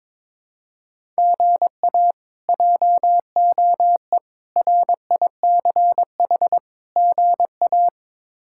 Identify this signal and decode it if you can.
Morse code, heard as a keyed tone, about 22 wpm, 700 Hz. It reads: GA JOE RICH GA
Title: K1USN SST QSO Practice - 22wpm